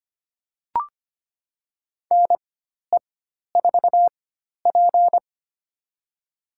Morse code, heard as a keyed tone, about 25 wpm, spaced as for 12 wpm, 700 Hz. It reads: NE4P
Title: USA Call Signs - 4 Characters - 4x Character Spacing - 25wpm